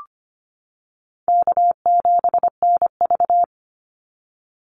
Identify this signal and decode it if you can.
Morse code, heard as a keyed tone, about 25 wpm, 700 Hz. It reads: K7NV
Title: Easy Call Signs - 25wpm